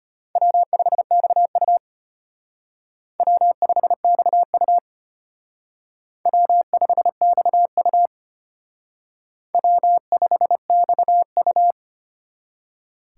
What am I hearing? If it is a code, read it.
Morse code, heard as a keyed tone, about 32 wpm, 700 Hz. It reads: W5XU W5XU W5XU W5XU